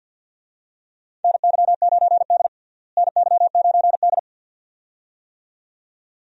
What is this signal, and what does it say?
Morse code, heard as a keyed tone, about 52 wpm, 700 Hz. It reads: NY9D NY9D
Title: USA Call Signs - 4 Characters - Repeated 2x - 50wpm